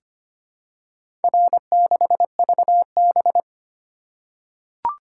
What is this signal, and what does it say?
Morse code, heard as a keyed tone, about 25 wpm, 700 Hz. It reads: R6VB
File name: practice / 25wpm / Four Letter-Number Combinations - 25wpm